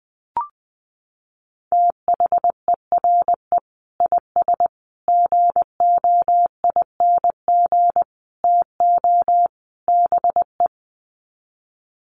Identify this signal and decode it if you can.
Morse code, heard as a keyed tone, about 20 wpm, 700 Hz. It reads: THERE IS GOING TO BE
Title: Sets of 5 Words Encore 20wpm